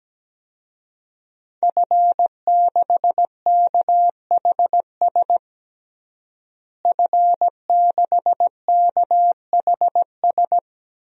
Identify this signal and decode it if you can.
Morse code, heard as a keyed tone, about 17 wpm, 700 Hz. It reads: F6KHS F6KHS
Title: DXCC-100 Call Signs - 5 Characters - Repeated 2x - 17wpm